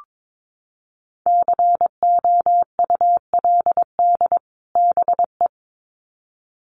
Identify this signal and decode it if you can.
Morse code, heard as a keyed tone, about 22 wpm, 700 Hz. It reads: COULD BE
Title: Sets of 2 Words 22wpm